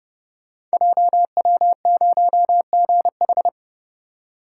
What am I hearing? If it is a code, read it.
Morse code, heard as a keyed tone, about 30 wpm, 700 Hz. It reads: JW0GH
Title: Code Groups - 5 Letter:Numbers - 30wpm